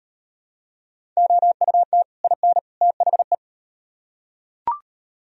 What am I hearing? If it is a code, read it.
Morse code, heard as a keyed tone, about 35 wpm, 700 Hz. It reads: OUT IN THE